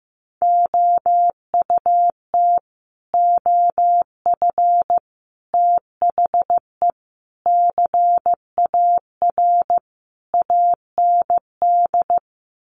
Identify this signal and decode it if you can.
Morse code, heard as a keyed tone, about 15 wpm, 700 Hz. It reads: OUT OF THE CAR AND
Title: Sets of 5 Words Encore 15wpm